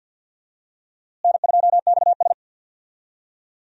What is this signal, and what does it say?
Morse code, heard as a keyed tone, about 51 wpm, 700 Hz. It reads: N2XS
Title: USA Call Signs - 4 Characters - Repeated 2x - 50wpm